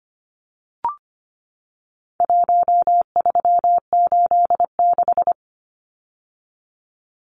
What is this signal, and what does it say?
Morse code, heard as a keyed tone, about 25 wpm, 700 Hz. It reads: 1386
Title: Numbers - Contest Style - 4-Digit 25wpm